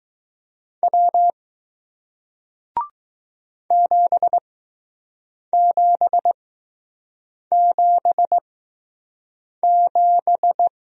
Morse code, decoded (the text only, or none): W 7777